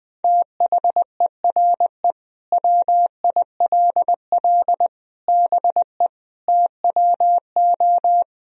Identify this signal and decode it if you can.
Morse code, heard as a keyed tone, about 20 wpm, 700 Hz. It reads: THERE WILL BE TWO
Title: Sentences from Top 1000 Words - Review Part 3 - 20wpm